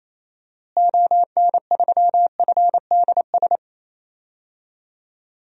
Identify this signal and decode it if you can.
Morse code, heard as a keyed tone, about 28 wpm, 700 Hz. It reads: ON3FDS
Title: DXCC-100 Call Signs - 6 Characters - Repeated 2x - 28wpm